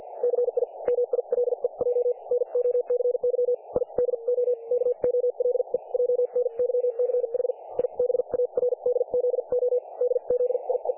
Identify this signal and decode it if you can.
Morse code, heard as a keyed tone, about 50 wpm, spaced as for 41 wpm, 500 Hz. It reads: CI MILEJ NOCY I DO KOLEJNYCH I BARDZO DZIE
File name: dlazaawansowanych / nagraniazpasma / websdr recording 3564.7kHz